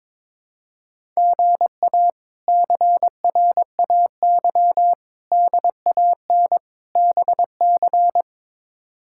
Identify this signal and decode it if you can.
Morse code, heard as a keyed tone, about 22 wpm, 700 Hz. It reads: GA CRAY DAN BC
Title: K1USN SST QSO Practice - 22wpm